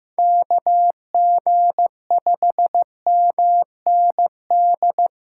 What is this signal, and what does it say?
Morse code, heard as a keyed tone, about 15 wpm, 700 Hz. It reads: KG5MND
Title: USA Callsign DE Callsign 15wpm